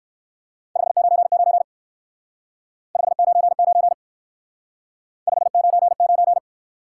Morse code, heard as a keyed tone, about 54 wpm, 700 Hz. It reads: 599 599 599
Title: Speed-Racing Top 180 QSO Elements 45wpm